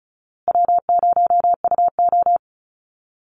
Morse code, heard as a keyed tone, about 34 wpm, 700 Hz. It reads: W0UO
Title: Call Signs Repeated 2x - 35wpm